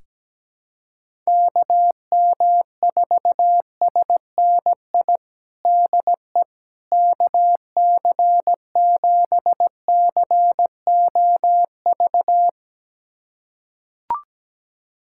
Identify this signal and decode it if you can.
Morse code, heard as a keyed tone, about 17 wpm, 700 Hz. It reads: KM4SNI DE KC7COV
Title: USA Callsign DE Callsign 17wpm